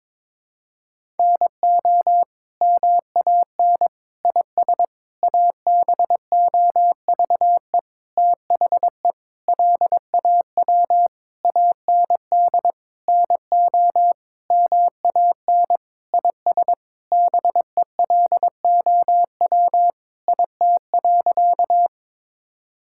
Morse code, heard as a keyed tone, about 22 wpm, 700 Hz. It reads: NO MAN IS ABOVE THE LAW AND NO MAN IS BELOW IT.
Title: Sentences from Top 1000 Words - Review Part 5 - 22wpm